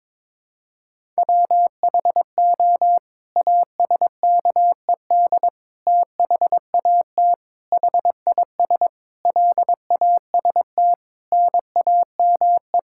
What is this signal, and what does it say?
Morse code, heard as a keyed tone, about 21 wpm, 700 Hz. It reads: WHO ASKED THAT HIS LAST NAME
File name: Sets of 9 Words - 22wpm